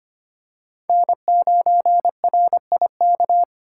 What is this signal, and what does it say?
Morse code, heard as a keyed tone, about 25 wpm, 700 Hz. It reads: N9RIK
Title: USA Call Signs - 5 Characters - 2x Character Spacing - 25wpm